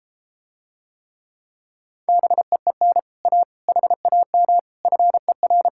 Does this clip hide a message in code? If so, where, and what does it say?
Yes: Morse code from 2.1 to 5.7 s, heard as a keyed tone, about 34 wpm, 700 Hz. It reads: BEEN A HAM FER